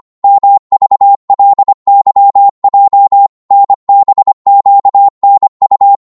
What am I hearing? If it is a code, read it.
Morse code, heard as a keyed tone, about 25 wpm, 800 Hz. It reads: MVLYJ NBQNU
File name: Morse / ltr-25wpm